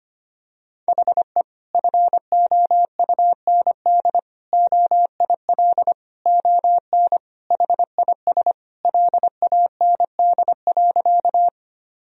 Morse code, heard as a keyed tone, about 25 wpm, 700 Hz. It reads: HE FOUND OIL ON HIS LAND.
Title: Sentences from Top 1000 Words - Review Part 4 - 25wpm